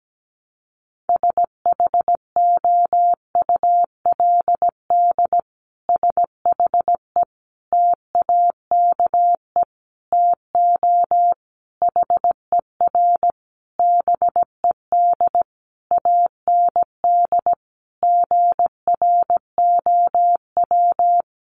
Morse code, heard as a keyed tone, about 17 wpm, 700 Hz. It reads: SHOULD SHE TAKE TO HER BED AND GROW
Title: Sentences from Top 1200 Words 17wpm